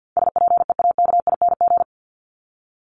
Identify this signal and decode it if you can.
Morse code, heard as a keyed tone, about 50 wpm, 700 Hz. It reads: SPEAKING
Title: ING Suffix - Top 100 Words with Base Words 50wpm